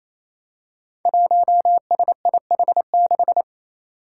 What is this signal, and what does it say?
Morse code, heard as a keyed tone, about 28 wpm, 700 Hz. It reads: 1SIH6
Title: Five Challenging Letter-Number Combinations 28wpm